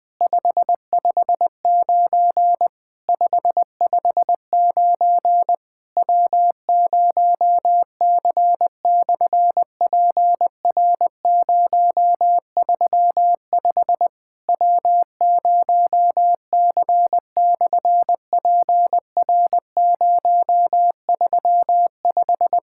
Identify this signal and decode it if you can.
Morse code, heard as a keyed tone, about 20 wpm, 700 Hz. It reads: 559 559 W0C/PR035 W0C/PR035
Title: SOTA QSO Practice 20wpm